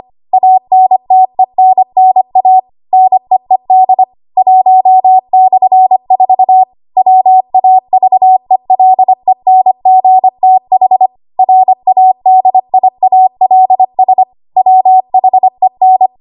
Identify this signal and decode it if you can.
Morse code, heard as a keyed tone, about 25 wpm, 750 Hz. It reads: ANTENNA NEED 1/4 WAVELENGTH RADIALS WHEN
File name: Archive / 25 WPM / 25WPM